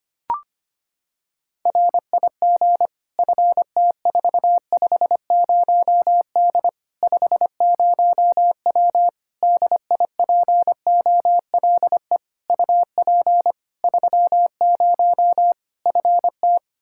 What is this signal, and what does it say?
Morse code, heard as a keyed tone, about 25 wpm, 700 Hz. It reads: RIG FT450D 50W DIPOLE UP 30 FT
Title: Practice QSO - Exchange 2 - Info - 25wpm